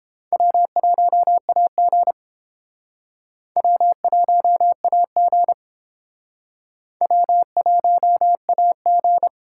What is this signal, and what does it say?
Morse code, heard as a keyed tone, about 30 wpm, 700 Hz. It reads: W1AG W1AG W1AG